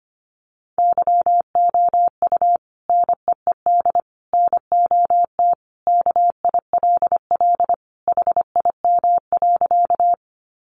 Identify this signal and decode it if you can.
Morse code, heard as a keyed tone, about 25 wpm, 700 Hz. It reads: YOU NEED NOT KILL HIM.
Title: Sentences from Top 1400 Words 25wpm